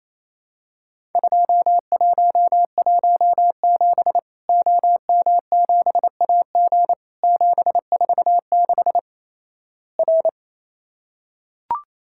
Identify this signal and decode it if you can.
Morse code, heard as a keyed tone, about 28 wpm, 700 Hz. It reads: 2117 OM7AG 746 R
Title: Worked All Europe DX Contest - QTC Exchange - 28wpm